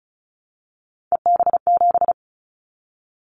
Morse code, heard as a keyed tone, about 34 wpm, 700 Hz. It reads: EB7
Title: Callsign Prefixes for Top 25 DXCC - 35wpm